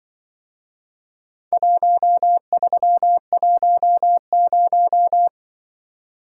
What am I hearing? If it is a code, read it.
Morse code, heard as a keyed tone, about 23 wpm, 700 Hz. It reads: 1310
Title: Speed-Racing - Four Letter-Number Combinations - 20wpm